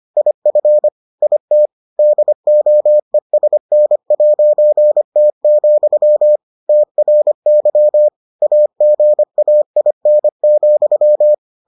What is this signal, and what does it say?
Morse code, heard as a keyed tone, about 25 wpm, 600 Hz. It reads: IF IT DOESN'T, TRY AGAIN,